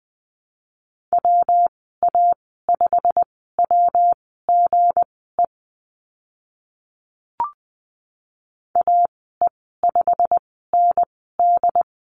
Morse code, heard as a keyed tone, about 20 wpm, spaced as for 14 wpm, 700 Hz. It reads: WA5WGE AE5ND
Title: 193 Lesson 31g - Letter v - Call Signs 20wpm